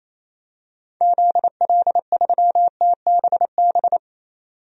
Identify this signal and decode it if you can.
Morse code, heard as a keyed tone, about 27 wpm, 700 Hz. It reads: ZL3TBB
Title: DXCC-100 Call Signs - 6 Characters - 28wpm